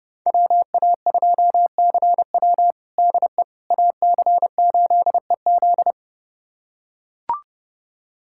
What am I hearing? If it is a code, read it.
Morse code, heard as a keyed tone, about 30 wpm, 700 Hz. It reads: WA2CW DE AC8EZ
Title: USA Callsign DE Callsign - 5 Characters - 30wpm